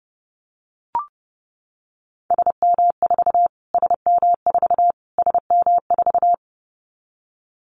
Morse code, heard as a keyed tone, about 30 wpm, 700 Hz. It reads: SM4 SM4 SM4
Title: Callsign Prefixes for Top 25 DXCC - Repeated 3x - 30wpm